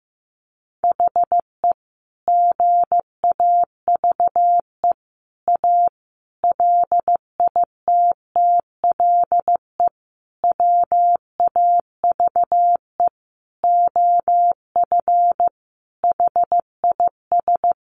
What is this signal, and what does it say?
Morse code, heard as a keyed tone, about 15 wpm, 700 Hz. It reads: HE GAVE A LITTLE WAVE OF HIS
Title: Sentences from Top 1400 Words 15wpm